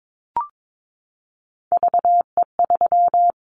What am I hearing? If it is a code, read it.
Morse code, heard as a keyed tone, about 22 wpm, 700 Hz. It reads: VE3
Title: Callsign Prefixes for Top 100 DXCC - 22wpm